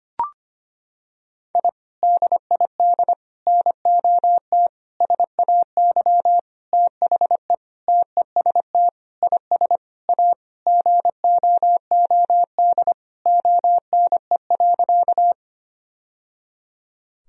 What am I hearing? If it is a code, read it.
Morse code, heard as a keyed tone, about 25 wpm, 700 Hz. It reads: I DID NOT SAY THE TEST IS A GOOD ONE.
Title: Sentences from Top 1000 Words - Review Part 3 - 25wpm